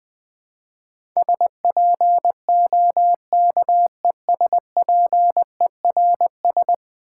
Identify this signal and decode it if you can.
Morse code, heard as a keyed tone, about 20 wpm, 700 Hz. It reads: SPOKESPERS
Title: Build A Word - 5 Letters At A Time - 12-Letter Words - 20wpm